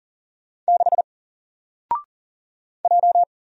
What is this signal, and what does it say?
Morse code, heard as a keyed tone, about 40 wpm, 700 Hz. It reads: 6 J
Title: Single Letter-Number 40wpm